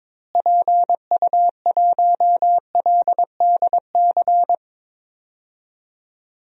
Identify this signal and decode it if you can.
Morse code, heard as a keyed tone, about 22 wpm, 700 Hz. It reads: PU1LDC
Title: DXCC-100 Call Signs - 6 Characters - Repeated 2x - 22wpm